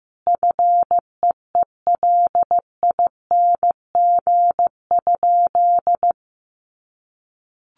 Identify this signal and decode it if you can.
Morse code, heard as a keyed tone, about 15 wpm, 700 Hz. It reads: FEELING?
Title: Sentences from Top 1200 Words 15wpm